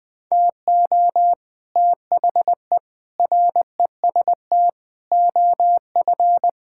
Morse code, heard as a keyed tone, about 20 wpm, 700 Hz. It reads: TO THE REST OF
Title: Build A Phrase - 5-Word Phrases - 20wpm